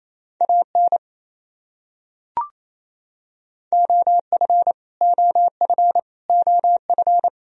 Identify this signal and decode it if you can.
Morse code, heard as a keyed tone, about 28 wpm, 700 Hz. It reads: AN OF OF OF
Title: Top 100 Words Repeated 28wpm